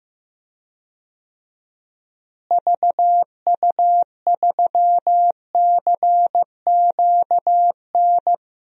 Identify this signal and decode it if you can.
Morse code, heard as a keyed tone, about 15 wpm, 700 Hz. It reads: VU3CQN
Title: DXCC-100 Call Signs - 6 Characters - Repeated 3x - 15wpm